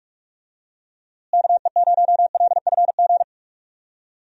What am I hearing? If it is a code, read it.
Morse code, heard as a keyed tone, about 42 wpm, 700 Hz. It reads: KE0LFG
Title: USA Call Signs - 6 Characters - Repeated 2x - 45wpm